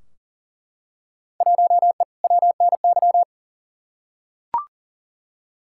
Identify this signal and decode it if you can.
Morse code, heard as a keyed tone, about 40 wpm, 700 Hz. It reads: 1E WNY
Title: Field Day Exchanges - 40wpm